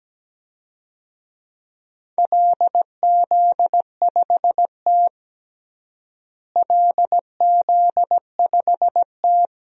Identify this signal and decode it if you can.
Morse code, heard as a keyed tone, about 17 wpm, 700 Hz. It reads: LZ5T LZ5T
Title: DXCC-100 Call Signs - 4 Characters - Repeated 2x - 17wpm